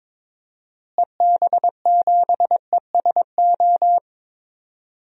Morse code, heard as a keyed tone, about 22 wpm, 700 Hz. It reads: EB7ESO
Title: DXCC-100 Call Signs - 6 Characters - Repeated 2x - 22wpm